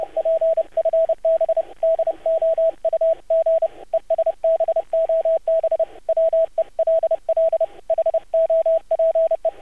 Morse code, heard as a keyed tone, about 30 wpm, 650 Hz, with radio noise. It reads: PFB DOUG ESBOB WELL HOPE
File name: archived / CRSnet